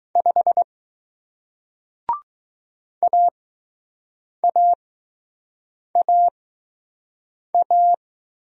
Morse code, heard as a keyed tone, about 20 wpm, spaced as for 6 wpm, 700 Hz. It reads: 5 AAAA